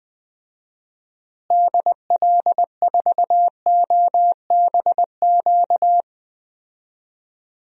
Morse code, heard as a keyed tone, about 21 wpm, 700 Hz. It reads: DL4OBQ
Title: USA Call Signs - 6 Characters - Repeated 3x - 20wpm